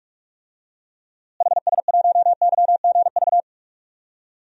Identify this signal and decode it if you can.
Morse code, heard as a keyed tone, about 44 wpm, 700 Hz. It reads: HS1YGV
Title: DXCC-100 Call Signs - 6 Characters - Repeated 2x - 45wpm